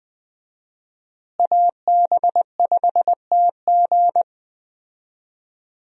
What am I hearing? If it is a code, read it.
Morse code, heard as a keyed tone, about 20 wpm, 700 Hz. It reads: AB5TG
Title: USA Call Signs - 5 Characters - Repeated 3x - 20wpm